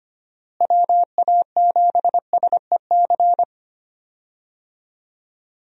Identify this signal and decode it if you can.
Morse code, heard as a keyed tone, about 25 wpm, 700 Hz. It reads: WA7SEC
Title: USA Call Signs - 6 Characters - Repeated 3x - 25wpm